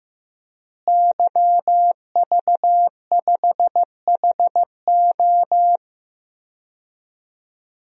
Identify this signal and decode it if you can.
Morse code, heard as a keyed tone, about 15 wpm, 700 Hz. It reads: YV5HO
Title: DXCC-100 Call Signs - 5 Characters - Repeated 2x - 15wpm